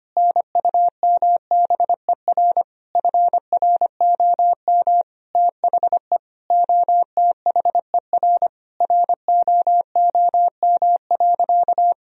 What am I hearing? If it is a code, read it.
Morse code, heard as a keyed tone, about 25 wpm, 700 Hz. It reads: NUMBER FROM THE OTHER ROOM.
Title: Sentences from Top 1000 Words - Review Part 1 - 25wpm